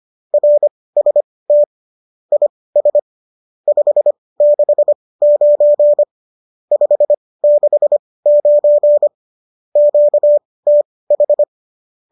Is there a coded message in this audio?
Yes: Morse code from 0.3 to 11.4 s, heard as a keyed tone, about 25 wpm, spaced as for 18 wpm, 590 Hz. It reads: RST IS 569 569 QTH